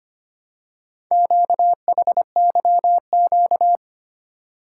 Morse code, heard as a keyed tone, about 25 wpm, 700 Hz. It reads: QHYQ
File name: practice / 25wpm / Four Letter-Number Combinations - 25wpm